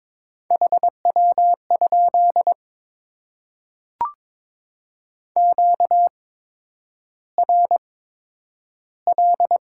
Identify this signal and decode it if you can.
Morse code, heard as a keyed tone, about 22 wpm, spaced as for 9 wpm, 700 Hz. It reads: HW? QRL
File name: Top 20 QSO Elements - 8x Character Spacing - 22wpm